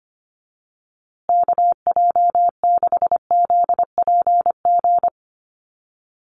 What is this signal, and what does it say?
Morse code, heard as a keyed tone, about 25 wpm, 700 Hz. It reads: KJ6ZPG
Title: USA Call Signs Repeated 3x 25wpm